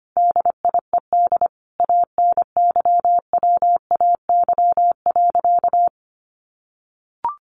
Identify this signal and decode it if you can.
Morse code, heard as a keyed tone, about 25 wpm, 700 Hz. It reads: DIED ANYWAY.
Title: A Sound Of Thunder 25wpm